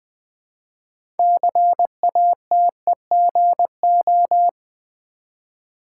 Morse code, heard as a keyed tone, about 20 wpm, 700 Hz. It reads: CATEGO
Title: Build A Word - 2 Letters At A Time - 8-Letter Words - 20wpm